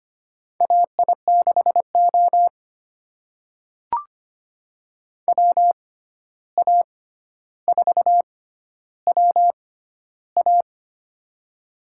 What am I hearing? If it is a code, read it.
Morse code, heard as a keyed tone, about 25 wpm, spaced as for 12 wpm, 700 Hz. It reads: AI6O WA4WA
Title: USA Call Signs - 6x Character Spacing - 25wpm